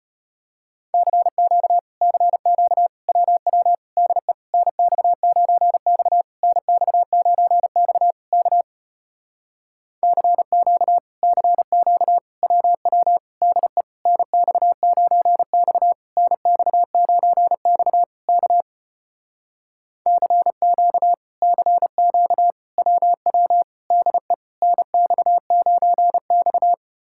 Morse code, heard as a keyed tone, about 33 wpm, 700 Hz. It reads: CQ CQ WW DE NX9X NX9X K CQ CQ WW DE NX9X NX9X K CQ CQ WW DE NX9X